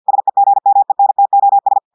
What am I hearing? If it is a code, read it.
Morse code, heard as a keyed tone, about 50 wpm, 800 Hz. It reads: SEGMENTOS
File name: portugais texte en clair 2 vitesse 50 mots minutes